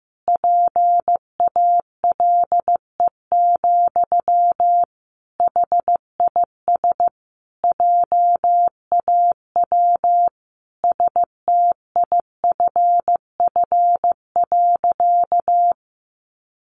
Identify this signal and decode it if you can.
Morse code, heard as a keyed tone, about 15 wpm, 700 Hz. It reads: PALE, HIS JAW STIFF.
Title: A Sound Of Thunder 15wpm